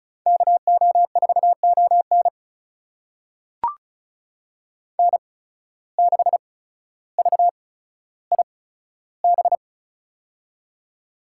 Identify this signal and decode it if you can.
Morse code, heard as a keyed tone, about 34 wpm, spaced as for 15 wpm, 700 Hz. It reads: KO4ON N6VIB
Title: USA Call Signs - 5 Characters - 8x Character Spacing - 35wpm